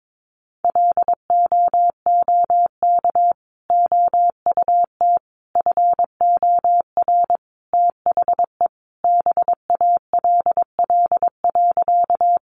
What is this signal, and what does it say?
Morse code, heard as a keyed tone, about 22 wpm, 700 Hz. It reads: LOOK OUT FOR THE BALL.
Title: Sentences from Top 900 Words 22wpm